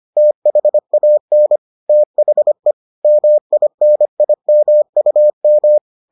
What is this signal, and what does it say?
Morse code, heard as a keyed tone, about 25 wpm, 600 Hz. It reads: THAN THE MINIMUM